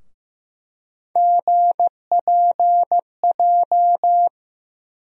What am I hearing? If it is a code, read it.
Morse code, heard as a keyed tone, about 15 wpm, 700 Hz. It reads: GPJ
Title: USA Call Sign Postfixes - 3 Letters 15wpm